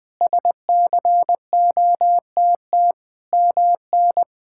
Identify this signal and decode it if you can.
Morse code, heard as a keyed tone, about 21 wpm, 700 Hz. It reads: SCOTT MN
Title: K1USN SST QSO Practice - 20wpm